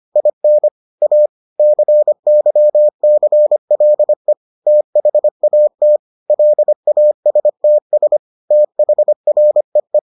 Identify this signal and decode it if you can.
Morse code, heard as a keyed tone, about 25 wpm, 600 Hz. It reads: IN A CYCLE THAT LASTS THREE